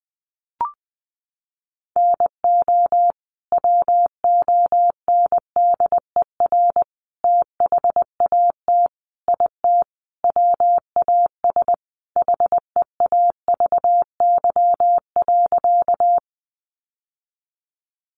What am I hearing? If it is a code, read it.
Morse code, heard as a keyed tone, about 20 wpm, 700 Hz. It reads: NO WONDER THAT IT WAS HEAVY.
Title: Sentences from Top 1100 Words 20wpm